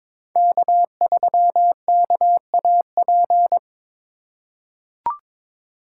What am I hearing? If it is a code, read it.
Morse code, heard as a keyed tone, about 22 wpm, 700 Hz. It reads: K3KAP
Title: USA Call Signs - 5 Characters - 2x Character Spacing - 22wpm